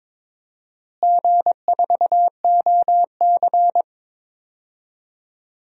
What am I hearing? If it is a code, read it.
Morse code, heard as a keyed tone, about 22 wpm, 700 Hz. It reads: G4OC
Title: DXCC-100 Call Signs - 4 Characters - Repeated 3x - 22wpm